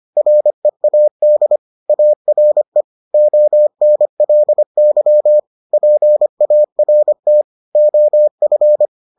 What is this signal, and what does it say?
Morse code, heard as a keyed tone, about 25 wpm, 600 Hz. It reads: READ ARE ONLY PART OF